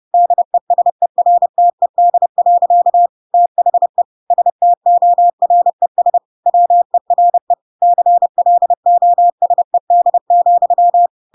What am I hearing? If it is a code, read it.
Morse code, heard as a keyed tone, about 30 wpm, 700 Hz. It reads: DESERTED. THE STORES WERE CLOSED,
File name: Pinocchio / Pinocchio ch6 30 30 ik1hge